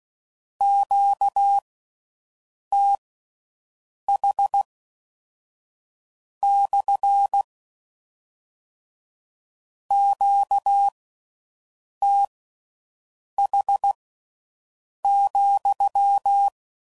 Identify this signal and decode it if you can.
Morse code, heard as a keyed tone, about 16 wpm, spaced as for 6 wpm, 800 Hz. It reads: QTH / QTH,